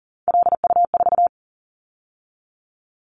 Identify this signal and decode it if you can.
Morse code, heard as a keyed tone, about 40 wpm, 700 Hz. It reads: LU4
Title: Callsign Prefixes for Top 100 DXCC - 40wpm